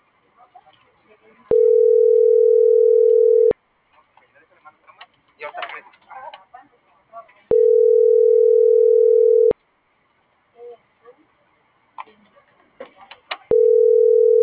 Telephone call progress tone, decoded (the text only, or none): ringback tone